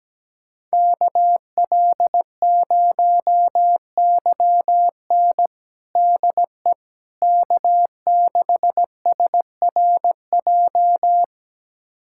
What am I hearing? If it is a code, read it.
Morse code, heard as a keyed tone, about 17 wpm, 700 Hz. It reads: KL0YN DE K6SRJ
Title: USA Callsign DE Callsign - 5 Characters - 17wpm